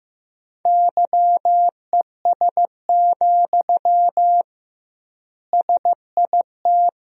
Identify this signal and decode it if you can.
Morse code, heard as a keyed tone, about 16 wpm, 700 Hz. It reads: YES, SIT
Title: 2x Word-Spacing - Sentences from Top 500 Words - 15wpm